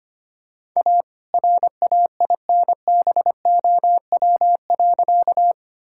Morse code, heard as a keyed tone, about 25 wpm, 700 Hz. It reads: A RAINBOW.
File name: practice / 25wpm / All-Age Appropriate Jokes - 25wpm